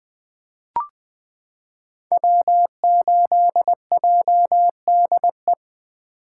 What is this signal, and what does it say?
Morse code, heard as a keyed tone, about 20 wpm, 700 Hz. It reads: W8JDE
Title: 2023 POTA Activator Call Signs - 20wpm